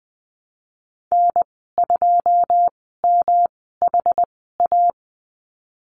Morse code, heard as a keyed tone, about 20 wpm, spaced as for 15 wpm, 700 Hz. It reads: N2MHA
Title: 122 Lesson 20g - Number 3 - Call Signs 20wpm